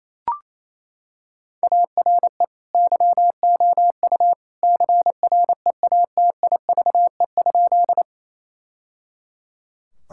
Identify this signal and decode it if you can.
Morse code, heard as a keyed tone, about 28 wpm, 700 Hz. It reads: ARE YOU CREATIVE?
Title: Sentences from Top 1000 Words - Review Part 5 - 28wpm